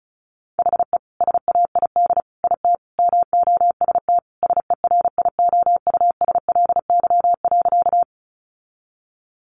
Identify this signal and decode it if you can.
Morse code, heard as a keyed tone, about 34 wpm, 700 Hz. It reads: HE SAID IT MOST SERIOUSLY.
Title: Sentences from Top 1200 Words 35wpm